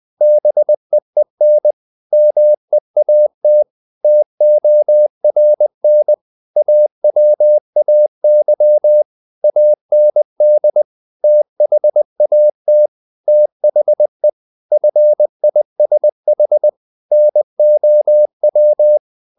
Morse code, heard as a keyed tone, about 20 wpm, 600 Hz. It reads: BEEN MEAT TORN AWAY AND THAT THE FISH NOW
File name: the old man and the sea part 2 20wpm